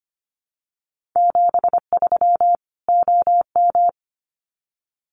Morse code, heard as a keyed tone, about 25 wpm, 700 Hz. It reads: 73 OM
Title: Common QSO Phrases 25wpm